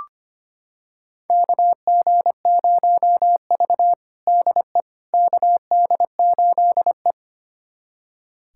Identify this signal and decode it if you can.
Morse code, heard as a keyed tone, about 25 wpm, 700 Hz. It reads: KG0V DE KD8E
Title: USA Callsign DE Callsign - 4 Characters - 25wpm